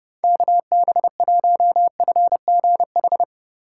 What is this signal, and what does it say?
Morse code, heard as a keyed tone, about 29 wpm, 700 Hz. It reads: KB1FGH